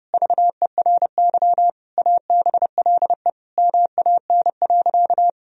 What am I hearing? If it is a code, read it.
Morse code, heard as a keyed tone, about 30 wpm, 700 Hz. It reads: VERY ABLE MAN.
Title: Sentences from Top 1000 Words - Review Part 2 - 30wpm